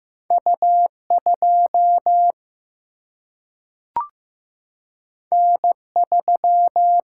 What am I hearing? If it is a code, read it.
Morse code, heard as a keyed tone, about 15 wpm, 700 Hz. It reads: U2 N3
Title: Single-Letter Single-Number 15wpm